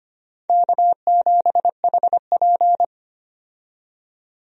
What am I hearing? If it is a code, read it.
Morse code, heard as a keyed tone, about 25 wpm, 700 Hz. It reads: K7HP
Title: USA Call Signs - 4 Characters - 25wpm